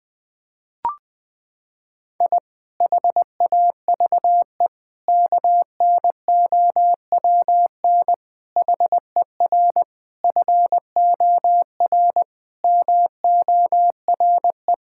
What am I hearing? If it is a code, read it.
Morse code, heard as a keyed tone, about 20 wpm, 700 Hz. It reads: I HAVE KNOWN HER FOR MORE
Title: Sentences with 5-Word Phrases - 20wpm